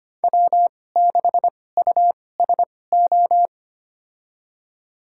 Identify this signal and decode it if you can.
Morse code, heard as a keyed tone, about 25 wpm, spaced as for 18 wpm, 700 Hz. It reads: W6USO
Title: USA Call Signs - 2x Character Spacing - 25wpm